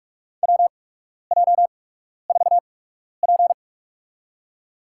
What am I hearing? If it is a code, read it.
Morse code, heard as a keyed tone, about 43 wpm, spaced as for 12 wpm, 700 Hz. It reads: WJ4P
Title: USA Call Signs - 4 Characters - 8x Character Spacing - 45wpm